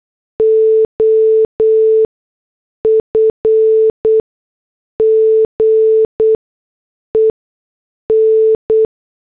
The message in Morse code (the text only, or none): OF GEN